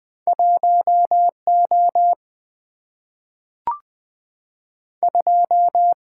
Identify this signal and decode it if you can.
Morse code, heard as a keyed tone, about 20 wpm, 700 Hz. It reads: 1O 2
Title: Single-Letter Single-Number 20wpm